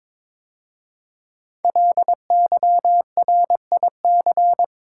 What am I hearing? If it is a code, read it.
Morse code, heard as a keyed tone, about 22 wpm, 700 Hz. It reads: LYRIC
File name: 5 Letter Words - First 3 Letters Given - 22wpm